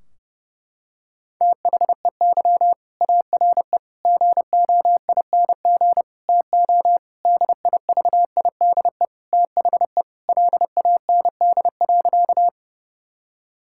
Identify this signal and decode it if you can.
Morse code, heard as a keyed tone, about 30 wpm, 700 Hz. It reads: THEY ARE GOING TO DIVIDE THE LAND.
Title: Sentences from Top 1600 Words 30wpm